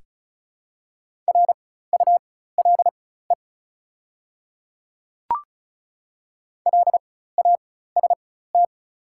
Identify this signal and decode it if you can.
Morse code, heard as a keyed tone, about 36 wpm, spaced as for 16 wpm, 700 Hz. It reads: RULE LAST
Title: 4x Character Spacing - Top 500 Words - 35wpm